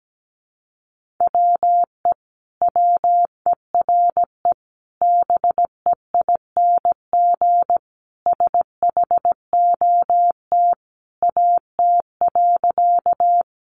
Morse code, heard as a keyed tone, about 17 wpm, 700 Hz. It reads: WE WERE BEING SHOT AT.
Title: Sentences from Top 1100 Words 17wpm